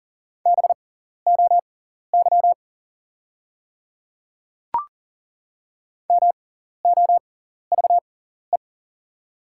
Morse code, heard as a keyed tone, about 40 wpm, spaced as for 14 wpm, 700 Hz. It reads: BOY MOVE
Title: 6x Character Spacing - Top 200 Words - 40wpm